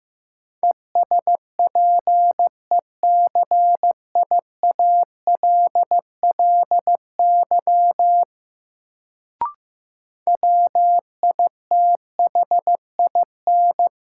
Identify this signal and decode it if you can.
Morse code, heard as a keyed tone, about 15 wpm, 700 Hz. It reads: ESPECIALLY WITHIN
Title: Sentences from Top 700 Words - New Word Drill - 15wpm